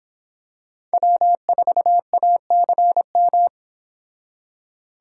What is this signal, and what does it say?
Morse code, heard as a keyed tone, about 25 wpm, 700 Hz. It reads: W4ACM